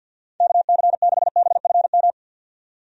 Morse code, heard as a keyed tone, about 50 wpm, 700 Hz. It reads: KC6BFM
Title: USA Call Signs - 6 Characters - 50wpm